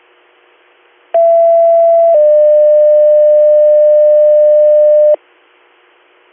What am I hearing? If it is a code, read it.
Two-tone page, heard as a one-second tone, then a three-second tone, about 670 then 600 Hz. It is heard from 1.1 to 5.1 s.